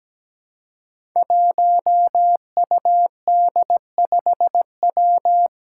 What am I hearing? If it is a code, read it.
Morse code, heard as a keyed tone, about 17 wpm, 700 Hz. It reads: 1UD5W
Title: Five Challenging Letter-Number Combinations 17wpm